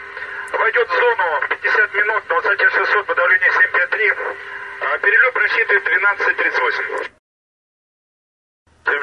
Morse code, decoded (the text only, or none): none